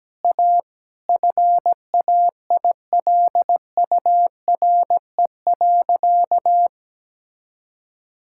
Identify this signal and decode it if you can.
Morse code, heard as a keyed tone, about 17 wpm, 700 Hz. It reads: A FAILURE.
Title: Sentences from Top 1000 Words - Review Part 6 - 17wpm